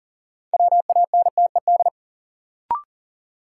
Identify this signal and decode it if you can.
Morse code, heard as a keyed tone, about 40 wpm, 700 Hz. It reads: WANTED
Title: Sets of 2 Words - New Word Drill - 40wpm